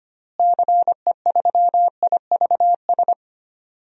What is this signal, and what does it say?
Morse code, heard as a keyed tone, about 25 wpm, 700 Hz. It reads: CE3IVS
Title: DXCC-100 Call Signs - 6 Characters - Repeated 2x - 25wpm